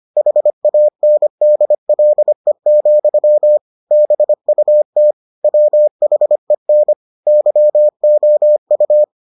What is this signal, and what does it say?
Morse code, heard as a keyed tone, about 25 wpm, 600 Hz. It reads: HANDLE, BUT WHEN YOU